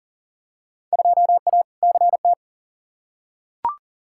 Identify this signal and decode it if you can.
Morse code, heard as a keyed tone, about 38 wpm, 700 Hz. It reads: 2A CT
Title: Field Day Exchanges Encore - 40wpm